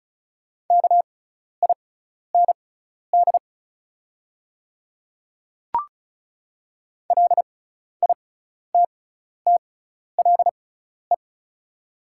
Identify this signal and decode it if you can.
Morse code, heard as a keyed tone, about 36 wpm, spaced as for 12 wpm, 700 Hz. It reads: KIND LITTLE
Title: 6x Character Spacing - Top 300 Words - 35wpm